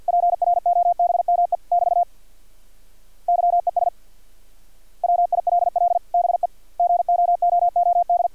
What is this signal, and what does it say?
Morse code, heard as a keyed tone, about 50 wpm, 700 Hz. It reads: PROBME <BT> YES WILL BE GOOOD
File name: archived / CRSnet